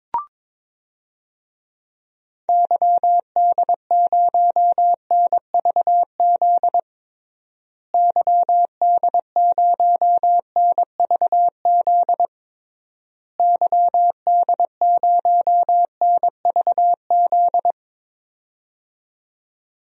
Morse code, heard as a keyed tone, about 22 wpm, 700 Hz. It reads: YD0NVZ YD0NVZ YD0NVZ
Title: DXCC-100 Call Signs - 6 Characters - Repeated 3x - 22wpm